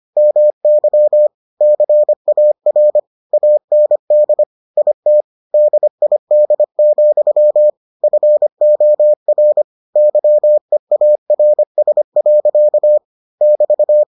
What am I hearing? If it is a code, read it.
Morse code, heard as a keyed tone, about 25 wpm, 600 Hz. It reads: MY CAR AND IT DID, FOR YEARS. <BT>